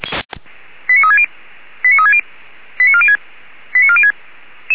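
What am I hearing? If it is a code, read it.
Five-tone selective call (ZVEI), heard as quick runs of five tones, tones of about 70 ms. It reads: 89360 89360 89406 89406
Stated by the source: ZVEI five-tone selective call received on an SDR.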